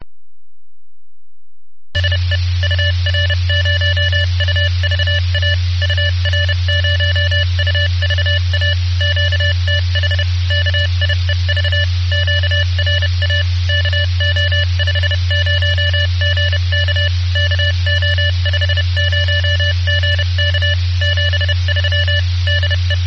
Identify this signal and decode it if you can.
Morse code, heard as a keyed tone, about 30 wpm, 1750 Hz, with noise. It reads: SE UR0UVA UR0UVA QTH KIEV QRA KO50GK KO50GK 73 DE